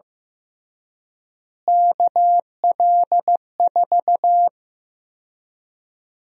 Morse code, heard as a keyed tone, about 15 wpm, 700 Hz. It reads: KL4
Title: USA Call Sign Prefixes - Repeated 2x - 15wpm